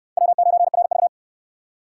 Morse code, heard as a keyed tone, about 68 wpm, 700 Hz. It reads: W9RV